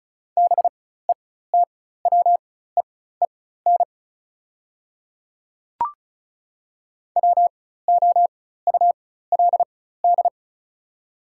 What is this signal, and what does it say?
Morse code, heard as a keyed tone, about 34 wpm, spaced as for 16 wpm, 700 Hz. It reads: BETWEEN WOULD
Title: 4x Character Spacing - Top 300 Words - 35wpm